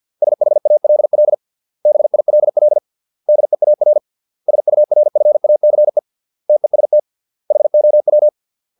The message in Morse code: SHALL BILL DEAR SURFACE TEST HOW

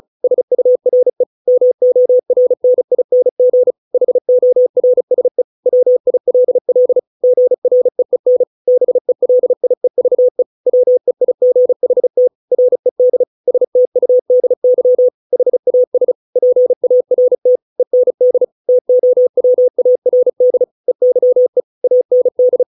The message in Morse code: SURE MORNING HORSE WILL GREEN BELIEVE WEIGHT RED STUDY HAS PART END TOWARD EYE AND

